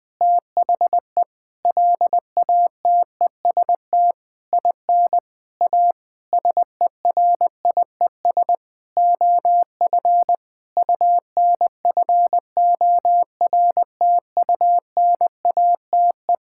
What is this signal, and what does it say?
Morse code, heard as a keyed tone, about 20 wpm, 700 Hz. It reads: THE LATEST IN A SERIES OF UNFORTUNATE
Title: Sentences with 8-Word Phrases - 20wpm